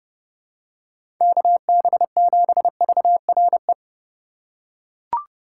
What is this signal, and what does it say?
Morse code, heard as a keyed tone, about 30 wpm, 700 Hz. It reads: KB7VRE
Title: USA Call Signs - 6 Characters - 2x Character Spacing - 30wpm